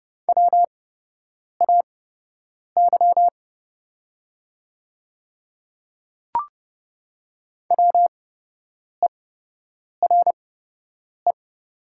Morse code, heard as a keyed tone, about 30 wpm, spaced as for 8 wpm, 700 Hz. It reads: WAY WERE
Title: 8x Character Spacing - Top 100 Words - 30wpm